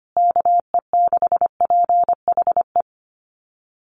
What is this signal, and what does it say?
Morse code, heard as a keyed tone, about 25 wpm, 700 Hz. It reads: KE6PHE
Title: USA Call Signs Repeated 3x 25wpm